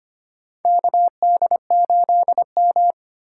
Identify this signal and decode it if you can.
Morse code, heard as a keyed tone, about 25 wpm, 700 Hz. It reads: KD8M
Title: USA Call Signs - 4x Character Spacing - 25wpm